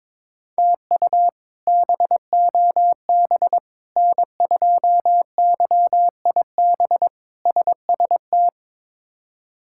Morse code, heard as a keyed tone, about 22 wpm, 700 Hz. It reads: TU BOB N2YIB SST
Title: K1USN SST QSO Practice - 22wpm